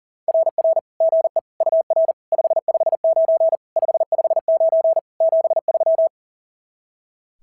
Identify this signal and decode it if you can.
Morse code, heard as a keyed tone, about 40 wpm, 690 Hz. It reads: RR GE UR 559 559 73
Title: SOTA QSO Practice 40wpm